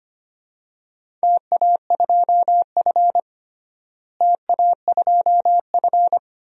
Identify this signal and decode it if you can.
Morse code, heard as a keyed tone, about 25 wpm, 700 Hz. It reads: TA2F TA2F
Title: DXCC-100 Call Signs - 4 Characters - Repeated 2x - 25wpm